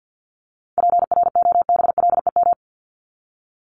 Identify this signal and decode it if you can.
Morse code, heard as a keyed tone, about 48 wpm, 700 Hz. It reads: PROBLEM
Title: ES:S Suffix - Top 100 Words with Base Words 50wpm